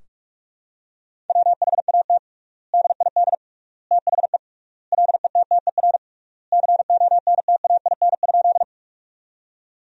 Morse code, heard as a keyed tone, about 46 wpm, 700 Hz. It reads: WHAT DID THE LETTER CONTAIN?
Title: 3x Word-Spacing - Sentences from Top 500 Words - 45wpm